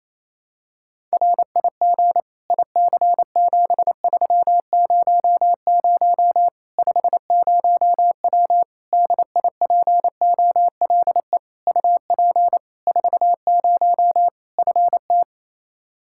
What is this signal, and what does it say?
Morse code, heard as a keyed tone, about 28 wpm, 700 Hz. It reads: RIG IC7300 50W DIPOLE UP 40 FT
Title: Practice QSO - Exchange 2 - Info - 28wpm